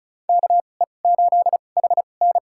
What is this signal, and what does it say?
Morse code, heard as a keyed tone, about 34 wpm, spaced as for 25 wpm, 700 Hz. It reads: KE8HN
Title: USA Call Signs - 5 Characters - 2x Character Spacing - 35wpm